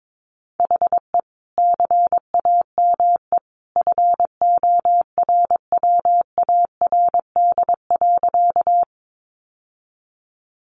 Morse code, heard as a keyed tone, about 22 wpm, 700 Hz. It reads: HE CAME FORWARD.
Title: Sentences from Top 1300 Words 22wpm